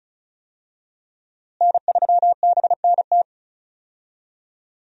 Morse code, heard as a keyed tone, about 36 wpm, 700 Hz. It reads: N3BNT
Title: USA Call Signs - 5 Characters - Repeated 2x - 35wpm